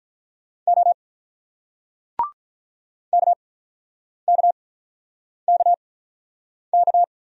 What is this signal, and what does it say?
Morse code, heard as a keyed tone, about 43 wpm, 700 Hz. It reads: O K K K K
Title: Speed-Racing Single Letters 35wpm